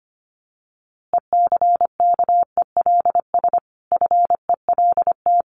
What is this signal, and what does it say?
Morse code, heard as a keyed tone, about 25 wpm, 700 Hz. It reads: ECKELS FELT
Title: A Sound Of Thunder 25wpm